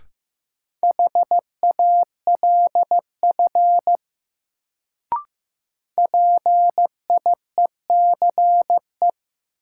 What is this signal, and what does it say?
Morse code, heard as a keyed tone, about 15 wpm, 700 Hz. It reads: HALF PIECE
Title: Sets of 3 Words - New Word Drill - 15wpm